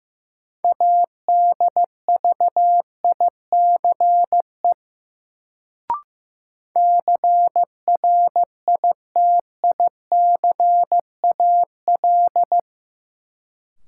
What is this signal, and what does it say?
Morse code, heard as a keyed tone, about 16 wpm, 700 Hz. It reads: ADVICE CRITICAL
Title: Sentences from Top 1100 Words - New Word Drill Alt - 15wpm